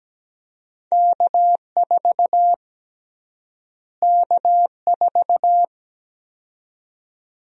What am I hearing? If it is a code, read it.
Morse code, heard as a keyed tone, about 17 wpm, 700 Hz. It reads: K4 K4
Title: USA Call Sign Prefixes - Repeated 2x - 17wpm